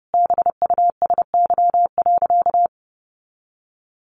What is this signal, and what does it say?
Morse code, heard as a keyed tone, about 30 wpm, 700 Hz. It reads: BUSY.
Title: Additional Easy Phrases 30wpm